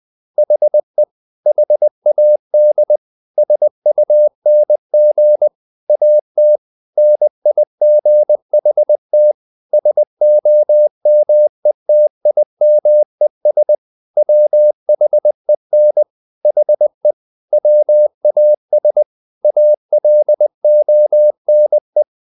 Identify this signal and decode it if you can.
Morse code, heard as a keyed tone, about 20 wpm, 600 Hz. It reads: HE HAD SUNG AT NIGHT SOMETIMES WHEN HE WAS ALONE
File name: the old man and the sea / the old man and the sea part 1 20wpm